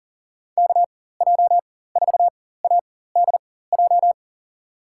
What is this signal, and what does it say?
Morse code, heard as a keyed tone, about 40 wpm, spaced as for 19 wpm, 700 Hz. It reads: KJ4ADJ
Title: USA Call Signs - 6 Characters - 4x Character Spacing - 40wpm